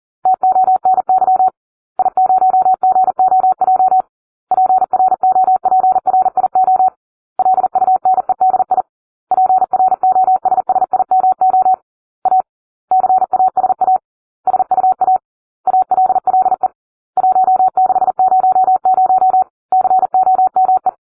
Morse code, heard as a keyed tone, about 40 wpm, 750 Hz. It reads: TONY I0GOJ PROPRIO LUNEDI PROSSIMO A CASA SUA ALLE 1600 COME